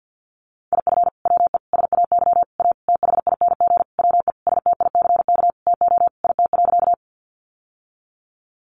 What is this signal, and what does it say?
Morse code, heard as a keyed tone, about 50 wpm, 700 Hz. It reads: IF WE SAY A THING WE STICK TO IT.
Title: Sentences from Top 1200 Words 50wpm